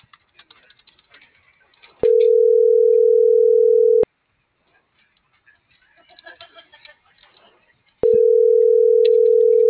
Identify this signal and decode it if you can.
Ringback tone (North America), 440 + 480 Hz.